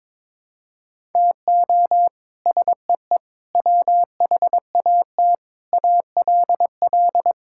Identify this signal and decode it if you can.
Morse code, heard as a keyed tone, about 22 wpm, 700 Hz. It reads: TO SEE WHAT ALL
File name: practice / 22wpm / Sets of 8 Words - 22wpm